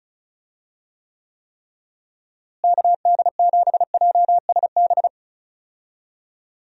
Morse code, heard as a keyed tone, about 36 wpm, 700 Hz. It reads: KD7JSB
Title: USA Call Signs - 6 Characters - Repeated 3x - 35wpm